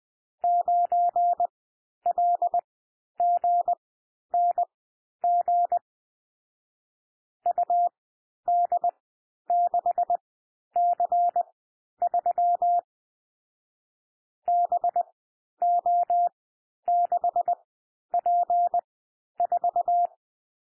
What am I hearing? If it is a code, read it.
Morse code, heard as a keyed tone, about 20 wpm, spaced as for 11 wpm, 700 Hz. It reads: 9LGNG UD6C3 BO6P4
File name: cwmp3 / M1 SOUND MIXED 20 wpm